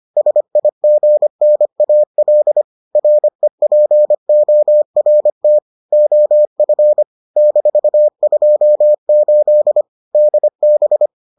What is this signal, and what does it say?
Morse code, heard as a keyed tone, about 25 wpm, 600 Hz. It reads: SIGNAL REPORT OF -28 DB